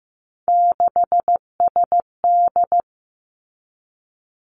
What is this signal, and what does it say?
Morse code, heard as a keyed tone, about 15 wpm, 700 Hz. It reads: 6SD
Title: Three Challenging Letter-Number Combinations 15wpm